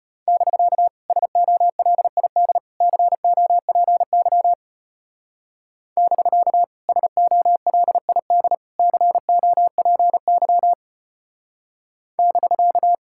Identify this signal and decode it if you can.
Morse code, heard as a keyed tone, about 30 wpm, 700 Hz. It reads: <BK> SOLID COPY <BK> SOLID COPY <BK>
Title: Speed-Racing Common QSO Phrases Encore - 25wpm